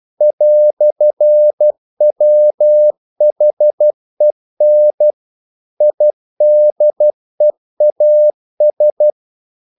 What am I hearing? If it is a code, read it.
Morse code, heard as a keyed tone, about 12 wpm, 600 Hz. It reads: "WHEN IDEAS
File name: CW / Practice / EnglishQuotations0312